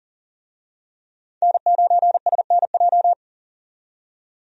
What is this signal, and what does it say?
Morse code, heard as a keyed tone, about 39 wpm, 700 Hz. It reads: N9SNJ
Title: USA Call Signs - 5 Characters - 40wpm